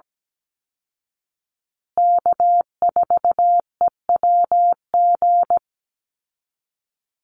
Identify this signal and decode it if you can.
Morse code, heard as a keyed tone, about 17 wpm, 700 Hz. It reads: K4EWG
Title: Call Signs Repeated 2x - 17wpm